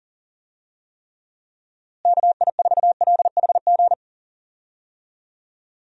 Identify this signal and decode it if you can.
Morse code, heard as a keyed tone, about 40 wpm, 700 Hz. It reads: KI4LHG
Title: USA Call Signs - 6 Characters - Repeated 2x - 40wpm